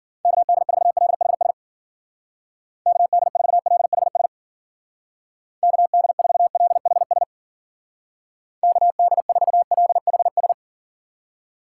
Morse code, heard as a keyed tone, about 51 wpm, 700 Hz. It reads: KD4LHS KD4LHS KD4LHS KD4LHS